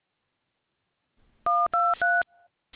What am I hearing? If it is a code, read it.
DTMF, heard as telephone touch tones, tones of about 220 ms. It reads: 1233